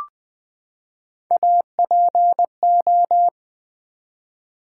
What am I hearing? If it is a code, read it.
Morse code, heard as a keyed tone, about 20 wpm, 700 Hz. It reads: APO
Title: Build A Word - 3 Letters At A Time - 9-Letter Words - 20wpm